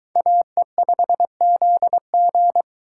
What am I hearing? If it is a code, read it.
Morse code, heard as a keyed tone, about 23 wpm, 700 Hz. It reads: AE5ZG